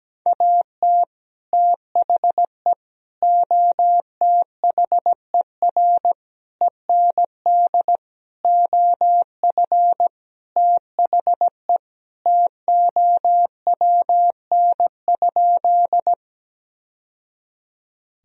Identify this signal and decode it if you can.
Morse code, heard as a keyed tone, about 17 wpm, 700 Hz. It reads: AT THE OTHER END OF THE TOWN?
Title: Sentences from Top 1000 Words - Review Part 2 - 17wpm